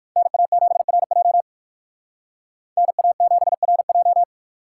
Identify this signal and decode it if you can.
Morse code, heard as a keyed tone, about 52 wpm, 700 Hz. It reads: NA7RJ NA7RJ
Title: Speed Racing - USA 2x2 Call Signs 45wpm